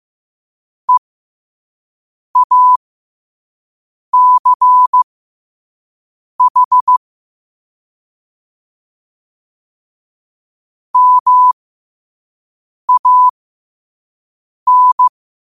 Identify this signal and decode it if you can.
Morse code, heard as a keyed tone, about 15 wpm, spaced as for 5 wpm, 1000 Hz. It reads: EACH MAN
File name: qotdfiles / 5wpm